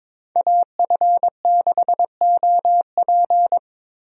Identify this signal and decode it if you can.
Morse code, heard as a keyed tone, about 21 wpm, 700 Hz. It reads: AF6OP
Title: USA Call Signs - 5 Characters - Repeated 2x - 22wpm